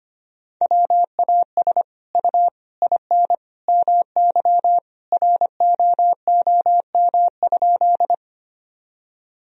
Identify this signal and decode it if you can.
Morse code, heard as a keyed tone, about 25 wpm, 700 Hz. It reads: WAS U IN MY ROOM?
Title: Sentences from Top 300 Words and Abbreviations 25wpm